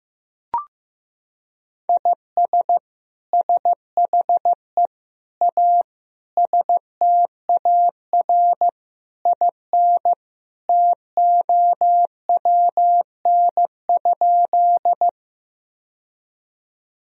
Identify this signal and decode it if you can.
Morse code, heard as a keyed tone, about 15 wpm, 700 Hz. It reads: IS SHE A STAR IN TOWN?
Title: Sentences from Top 1000 Words - Review Part 2 - 15wpm